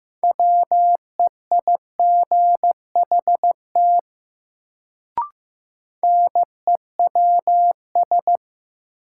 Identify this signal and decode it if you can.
Morse code, heard as a keyed tone, about 15 wpm, 700 Hz. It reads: WEIGHT NEWS
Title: Sentences from Top 800 Words - New Word Drill - 15wpm